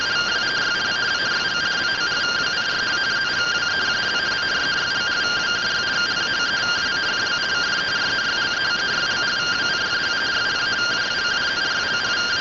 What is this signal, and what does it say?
Two-tone teleprinter signal, about 170 Hz shift.